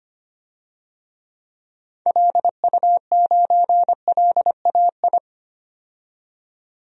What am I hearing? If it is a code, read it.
Morse code, heard as a keyed tone, about 25 wpm, 700 Hz. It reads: LU9LAI
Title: DXCC-100 Call Signs - 6 Characters - 25wpm